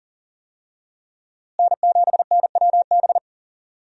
Morse code, heard as a keyed tone, about 39 wpm, 700 Hz. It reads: N7NWB
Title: USA Call Signs - 5 Characters - Repeated 2x - 40wpm